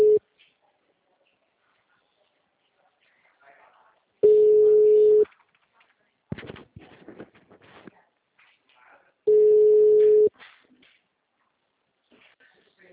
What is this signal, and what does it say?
Ringback tone (Europe, 425 Hz), with speech.
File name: out- -2012- - - .2170